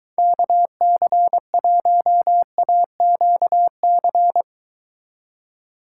Morse code, heard as a keyed tone, about 23 wpm, 700 Hz. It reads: KC1AQC